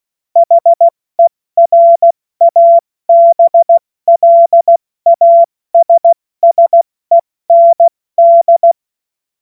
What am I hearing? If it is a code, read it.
Morse code, heard as a keyed tone, about 16 wpm, 680 Hz. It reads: HERABLASSEND
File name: Morsekurs / Witze / Witz 31. August 2021, 70 F80